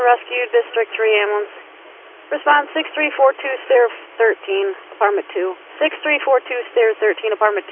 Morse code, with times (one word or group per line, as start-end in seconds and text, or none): none